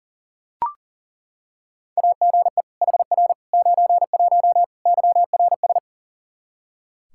Morse code, heard as a keyed tone, about 40 wpm, 700 Hz. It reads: AGE HR 91 YRS
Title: K1USN SST QSO Practice 40wpm